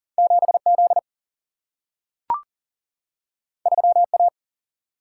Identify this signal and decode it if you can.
Morse code, heard as a keyed tone, about 40 wpm, 700 Hz. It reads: 7 Z 3 A
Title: Single-Letter Single-Number 40wpm